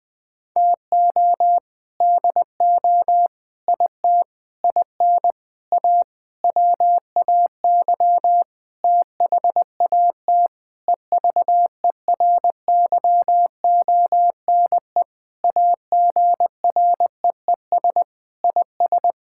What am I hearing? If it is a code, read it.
Morse code, heard as a keyed tone, about 20 wpm, 700 Hz. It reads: TO DO IT IN A WAY THAT EVERYONE AGREES IS
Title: Sentences with 7-Word Phrases - Spoken Beginning - 20wpm